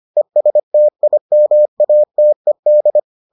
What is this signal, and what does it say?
Morse code, heard as a keyed tone, about 25 wpm, 600 Hz. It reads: ESTIMATED